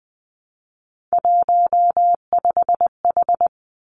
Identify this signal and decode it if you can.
Morse code, heard as a keyed tone, about 20 wpm, 700 Hz. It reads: 15H
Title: Three Challenging Letter-Number Combinations 20wpm